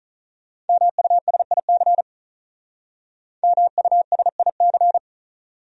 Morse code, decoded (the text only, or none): MUSIC MUSIC